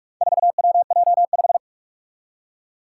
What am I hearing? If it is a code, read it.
Morse code, heard as a keyed tone, about 46 wpm, 700 Hz. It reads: 4WJ5
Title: Four Challenging Letter-Number Combinations 45wpm